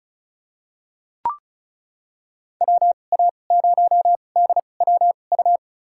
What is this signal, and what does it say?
Morse code, heard as a keyed tone, about 34 wpm, spaced as for 25 wpm, 700 Hz. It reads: WA0DWU
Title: USA Call Signs - 6 Characters - 2x Character Spacing - 35wpm